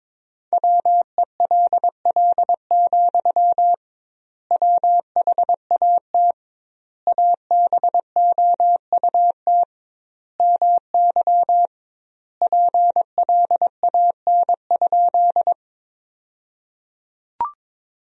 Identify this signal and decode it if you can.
Morse code, heard as a keyed tone, about 22 wpm, 700 Hz. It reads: WELL, WHAT ABOUT MY PLAN?
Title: 2x Word-Spacing - Sentences from Top 500 Words - 22wpm